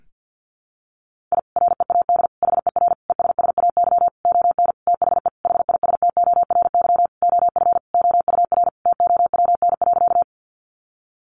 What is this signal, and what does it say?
Morse code, heard as a keyed tone, about 50 wpm, 700 Hz. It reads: I READ HER ESSAY ON THE HISTORY OF OUR TOWN.
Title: Sentences from Top 1500 Words 50wpm